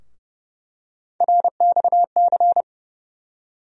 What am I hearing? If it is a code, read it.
Morse code, heard as a keyed tone, about 31 wpm, 700 Hz. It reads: RXC
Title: USA Call Sign Postfixes - 3 Letters 30wpm